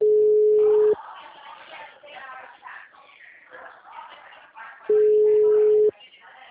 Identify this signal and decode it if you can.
Ringback tone (Europe, 425 Hz), with speech.
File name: out- -2003- - - .290